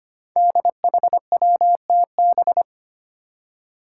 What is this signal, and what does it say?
Morse code, heard as a keyed tone, about 25 wpm, 700 Hz. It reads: DHWTB
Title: Five Challenging Letter-Number Combinations 25wpm